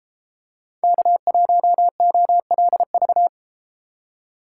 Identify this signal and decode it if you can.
Morse code, heard as a keyed tone, about 33 wpm, 700 Hz. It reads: K1OLV